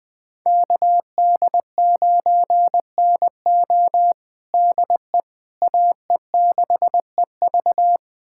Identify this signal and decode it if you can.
Morse code, heard as a keyed tone, about 20 wpm, 700 Hz. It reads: KD9NO DE AE6EV
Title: USA Callsign DE Callsign - 5 Characters - 20wpm